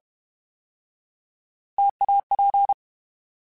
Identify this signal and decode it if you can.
Morse code, heard as a keyed tone, about 32 wpm, 800 Hz. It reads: TAP
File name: Alphabetical / Alphabetical mp3s / Ts @30